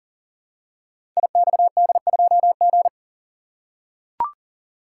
Morse code, heard as a keyed tone, about 42 wpm, 700 Hz. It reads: IXD2G
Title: Code Groups - 5 Letter:Numbers - 40wpm